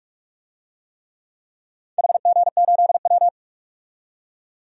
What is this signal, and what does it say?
Morse code, heard as a keyed tone, about 46 wpm, 700 Hz. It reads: HG8W
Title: DXCC-100 Call Signs - 4 Characters - 45wpm